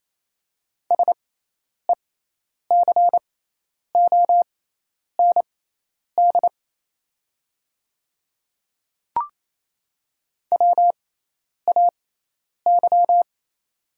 Morse code, heard as a keyed tone, about 28 wpm, spaced as for 10 wpm, 700 Hz. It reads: SECOND WAY
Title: 6x Character Spacing - Top 300 Words - 28wpm